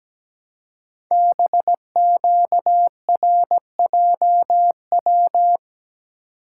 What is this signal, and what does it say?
Morse code, heard as a keyed tone, about 17 wpm, 700 Hz. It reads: BQRJW
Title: Code Groups - 5 Letters - 17wpm